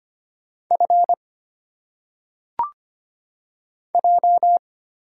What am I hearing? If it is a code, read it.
Morse code, heard as a keyed tone, about 25 wpm, 700 Hz. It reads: F J
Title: Single Letters - Phonetic Pronunciation 25wpm